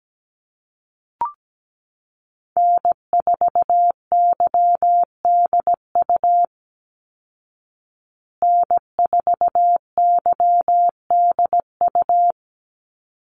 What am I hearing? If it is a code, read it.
Morse code, heard as a keyed tone, about 17 wpm, 700 Hz. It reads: N4YDU N4YDU
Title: Call Signs Repeated 3x - 17wpm